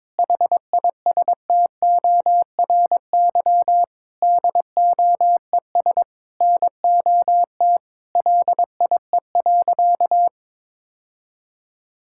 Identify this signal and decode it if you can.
Morse code, heard as a keyed tone, about 22 wpm, 700 Hz. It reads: HISTORY DOES NOT LIE.
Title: Sentences from Top 1600 Words 22wpm